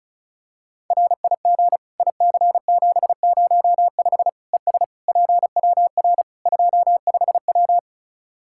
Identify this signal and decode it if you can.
Morse code, heard as a keyed tone, about 34 wpm, 700 Hz. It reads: RIG IC705 ES PWR 25W
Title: Practice QSO - Exchange 2 - Info - 35wpm